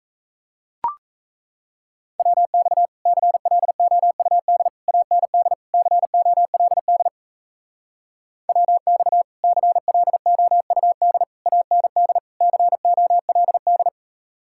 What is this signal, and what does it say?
Morse code, heard as a keyed tone, about 38 wpm, 700 Hz. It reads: WX CLOUD AND COLD WX CLOUD AND COLD